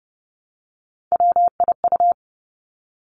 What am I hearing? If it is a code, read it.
Morse code, heard as a keyed tone, about 32 wpm, 700 Hz. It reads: WIU
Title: Three Challenging Letter-Number Combinations 30wpm